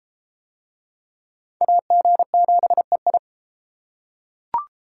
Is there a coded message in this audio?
Yes: Morse code from 1.6 to 3.2 s, heard as a keyed tone, about 30 wpm, 700 Hz. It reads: AG7EI